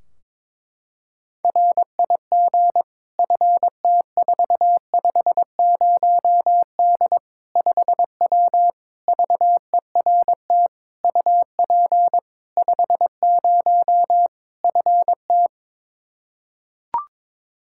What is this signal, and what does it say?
Morse code, heard as a keyed tone, about 22 wpm, 700 Hz. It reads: RIG FT450D 5W VERT UP 50 FT
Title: Practice QSO - Exchange 2 - Info - 22wpm